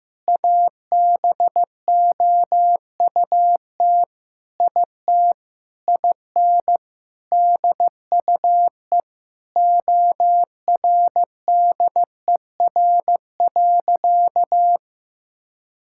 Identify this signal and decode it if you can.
Morse code, heard as a keyed tone, about 15 wpm, 700 Hz. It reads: ABOUT IT IN DUE ORDER.
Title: Sentences from Top 1000 Words - Review Part 6 - 15wpm